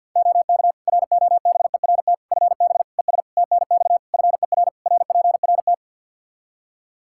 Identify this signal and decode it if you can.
Morse code, heard as a keyed tone, about 48 wpm, 700 Hz. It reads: OK ROBERT FB ES TNX FER RPRT
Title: K1USN SST QSO Practice 50wpm